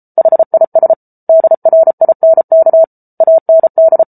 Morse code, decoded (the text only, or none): HIS DRINK AND